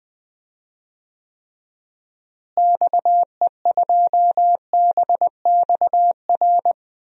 Morse code, heard as a keyed tone, about 21 wpm, 700 Hz. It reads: XE2BXR
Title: DXCC-100 Call Signs - 6 Characters - Repeated 3x - 20wpm